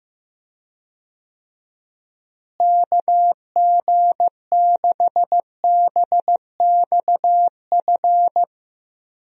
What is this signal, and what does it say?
Morse code, heard as a keyed tone, about 15 wpm, 700 Hz. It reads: KG6BXF
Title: USA Call Signs - 6 Characters - Repeated 3x - 15wpm